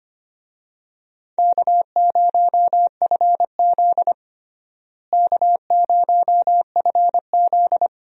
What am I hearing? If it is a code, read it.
Morse code, heard as a keyed tone, about 25 wpm, 700 Hz. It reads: K0FZ K0FZ
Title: USA Call Signs - 4 Characters - Repeated 2x - 25wpm